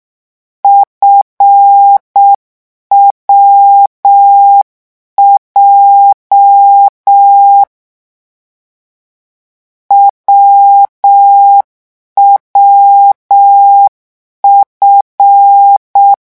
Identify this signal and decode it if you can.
Morse code, heard as a keyed tone, about 6 wpm, 800 Hz. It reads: FWJ WWF
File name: morse mp3 / lecon7-vitesse 6